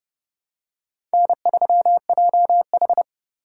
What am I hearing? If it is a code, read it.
Morse code, heard as a keyed tone, about 28 wpm, 700 Hz. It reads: N3JH